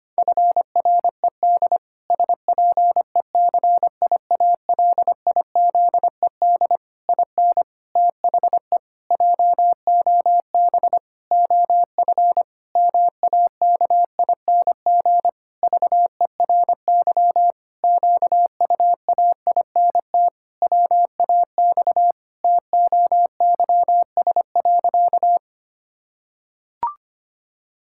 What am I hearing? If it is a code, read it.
Morse code, heard as a keyed tone, about 25 wpm, 700 Hz. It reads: FRED SPECIALIZED IN THE JOB OF MAKING VERY QUAINT WAX TOYS.
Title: Pangram Challenge v2 - 25wpm